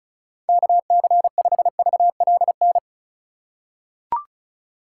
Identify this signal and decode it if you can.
Morse code, heard as a keyed tone, about 36 wpm, 700 Hz. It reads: KC5VLN
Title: USA Call Signs - 6 Characters - 3x Character Spacing - 35wpm